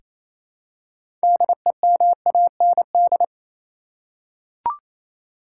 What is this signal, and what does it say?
Morse code, heard as a keyed tone, about 28 wpm, 700 Hz. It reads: DEMAND
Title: Six Letter Words - 28wpm